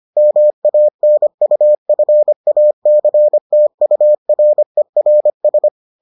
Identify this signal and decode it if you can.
Morse code, heard as a keyed tone, about 25 wpm, 600 Hz. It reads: MANUFACTURERS